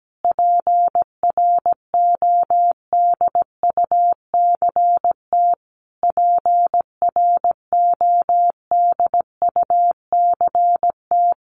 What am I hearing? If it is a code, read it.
Morse code, heard as a keyed tone, about 17 wpm, 700 Hz. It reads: PRODUCT PRODUCT
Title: Top 500 Words Repeated 17wpm